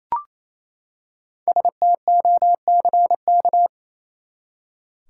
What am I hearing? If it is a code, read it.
Morse code, heard as a keyed tone, about 28 wpm, 700 Hz. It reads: STOCK
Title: Five Letter Words - 28wpm